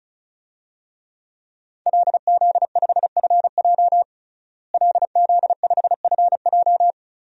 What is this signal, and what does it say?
Morse code, heard as a keyed tone, about 36 wpm, 700 Hz. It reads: LZ5FJ LZ5FJ
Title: DXCC-100 Call Signs - 5 Characters - Repeated 2x - 35wpm